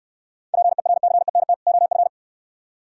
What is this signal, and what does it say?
Morse code, heard as a keyed tone, about 68 wpm, 690 Hz. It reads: PERCENT OF